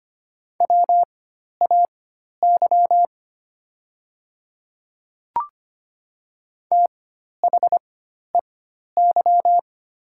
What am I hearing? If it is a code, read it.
Morse code, heard as a keyed tone, about 25 wpm, spaced as for 12 wpm, 700 Hz. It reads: WAY THEY
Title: 4x Character Spacing - Top 100 Words - 25wpm